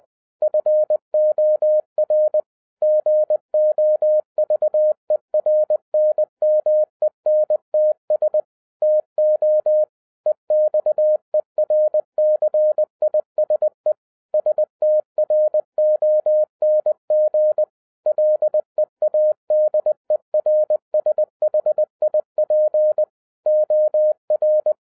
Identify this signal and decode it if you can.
Morse code, heard as a keyed tone, about 20 wpm, 610 Hz. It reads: FOR GOVERNMENTS TO EXERCISE STRONG LEADERSHIP OR